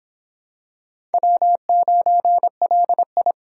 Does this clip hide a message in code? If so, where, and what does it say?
Yes: Morse code from 1.1 to 3.3 s, heard as a keyed tone, about 26 wpm, 700 Hz. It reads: W9LI